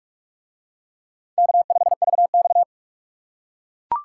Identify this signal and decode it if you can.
Morse code, heard as a keyed tone, about 44 wpm, 700 Hz. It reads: K5VX